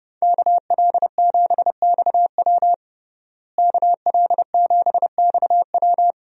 Morse code, heard as a keyed tone, about 30 wpm, 700 Hz. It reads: KL7XW KL7XW
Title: DXCC-100 Call Signs - 5 Characters - Repeated 2x - 30wpm